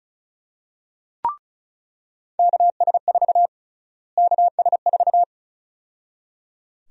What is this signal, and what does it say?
Morse code, heard as a keyed tone, about 34 wpm, 700 Hz. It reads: KS4 KS4
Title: USA Call Sign Prefixes - Repeated 2x - 35wpm